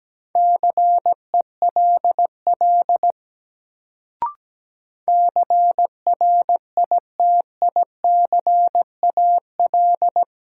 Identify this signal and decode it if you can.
Morse code, heard as a keyed tone, about 17 wpm, 700 Hz. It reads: CELL CRITICAL
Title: Sentences from Top 1100 Words - New Word Drill Alt - 17wpm